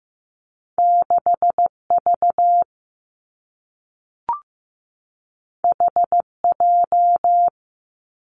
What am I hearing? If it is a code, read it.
Morse code, heard as a keyed tone, about 15 wpm, 700 Hz. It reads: 6 V H J
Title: Two Challenging Letter-Number Combinations 15wpm